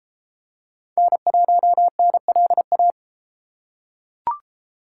Morse code, heard as a keyed tone, about 33 wpm, 700 Hz. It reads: N1NLA